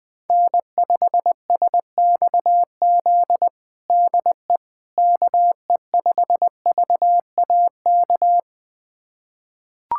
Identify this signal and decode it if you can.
Morse code, heard as a keyed tone, about 21 wpm, 700 Hz. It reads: N5SXZ DE KE5VAK
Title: USA Callsign DE Callsign 20wpm